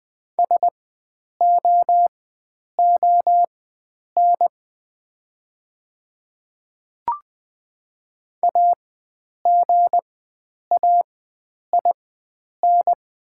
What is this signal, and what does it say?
Morse code, heard as a keyed tone, about 20 wpm, spaced as for 9 wpm, 700 Hz. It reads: SOON AGAIN
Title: 4x Character Spacing - Top 500 Words - 20wpm